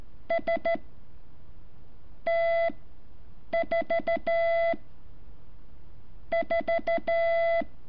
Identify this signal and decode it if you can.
Morse code, heard as a keyed tone, about 14 wpm, 670 Hz. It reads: S T 4 4
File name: curso / ejercicio 11